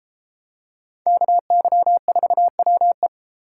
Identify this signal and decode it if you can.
Morse code, heard as a keyed tone, about 28 wpm, 700 Hz. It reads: KY4WE